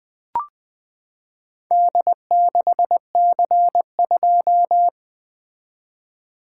Morse code, heard as a keyed tone, about 20 wpm, 700 Hz. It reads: D6C2
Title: Four Letter-Number Combinations - 20wpm